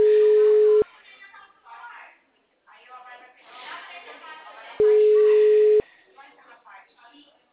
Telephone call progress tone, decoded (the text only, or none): ringback tone